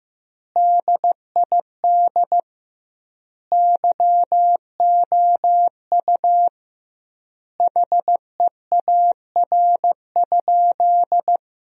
Morse code, heard as a keyed tone, about 15 wpm, 700 Hz. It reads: DID YOU HEAR?
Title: 2x Word-Spacing - Sentences from Top 300 Words - 15wpm